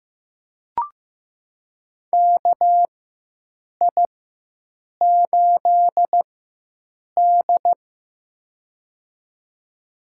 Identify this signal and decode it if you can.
Morse code, heard as a keyed tone, about 15 wpm, spaced as for 7 wpm, 700 Hz. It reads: KI8D
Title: USA Call Signs - 4 Characters - 4x Character Spacing - 15wpm